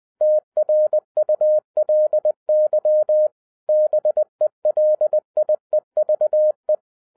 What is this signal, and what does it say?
Morse code, heard as a keyed tone, about 20 wpm, 610 Hz. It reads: TRULY BELIEVE